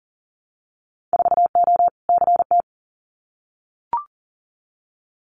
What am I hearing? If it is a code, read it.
Morse code, heard as a keyed tone, about 40 wpm, 700 Hz. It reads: 4O CT
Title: Winter Field Day Exchanges Encore 40wpm